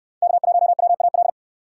Morse code, heard as a keyed tone, about 66 wpm, 700 Hz. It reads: K1WAL